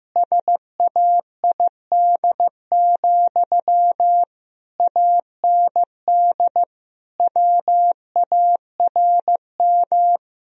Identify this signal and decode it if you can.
Morse code, heard as a keyed tone, about 15 wpm, 700 Hz. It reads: SAID, AND WARM
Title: Sentences from Top 1000 Words - Review Part 2 - 15wpm